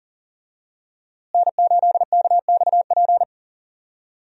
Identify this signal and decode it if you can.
Morse code, heard as a keyed tone, about 39 wpm, 700 Hz. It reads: N8KXP
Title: USA Call Signs - 5 Characters - Repeated 3x - 40wpm